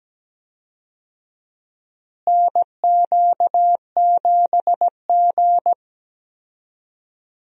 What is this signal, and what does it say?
Morse code, heard as a keyed tone, about 17 wpm, 700 Hz. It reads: NQ7G
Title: USA Call Signs - 4 Characters - Repeated 3x - 17wpm